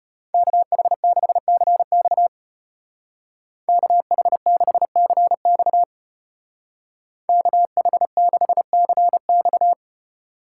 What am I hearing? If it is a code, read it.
Morse code, heard as a keyed tone, about 30 wpm, 700 Hz. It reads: KH6CX KH6CX KH6CX
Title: Speed Racing - USA 2x2 Call Signs 25wpm